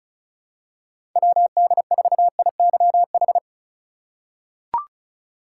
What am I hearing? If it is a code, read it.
Morse code, heard as a keyed tone, about 34 wpm, 700 Hz. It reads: WD4IYH
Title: USA Call Signs - 6 Characters - 4x Character Spacing - 35wpm